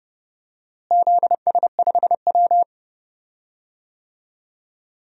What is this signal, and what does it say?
Morse code, heard as a keyed tone, about 30 wpm, 700 Hz. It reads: ZS5W
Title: DXCC-100 Call Signs - 4 Characters - Repeated 2x - 30wpm